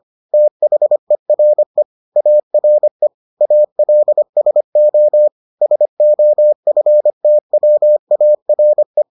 Morse code, heard as a keyed tone, about 25 wpm, 600 Hz. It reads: THERE ARE ALSO SOFTWARE